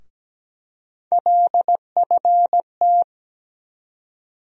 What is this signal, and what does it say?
Morse code, heard as a keyed tone, about 17 wpm, 700 Hz. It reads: LFT
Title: USA Call Sign Postfixes - 3 Letters 17wpm